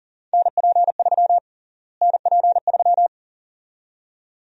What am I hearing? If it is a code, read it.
Morse code, heard as a keyed tone, about 40 wpm, 700 Hz. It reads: NP3 NP3
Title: USA Call Sign Prefixes - Repeated 2x - 40wpm